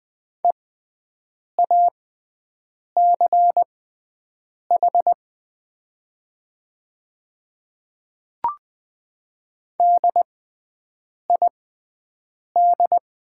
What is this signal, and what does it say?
Morse code, heard as a keyed tone, about 20 wpm, spaced as for 7 wpm, 700 Hz. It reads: EACH DID
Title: 6x Character Spacing - Top 100 Words - 20wpm